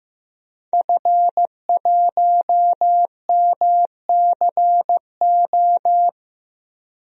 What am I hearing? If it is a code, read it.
Morse code, heard as a keyed tone, about 15 wpm, 700 Hz. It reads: F1MCO
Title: DXCC-100 Call Signs - 5 Characters - Repeated 2x - 15wpm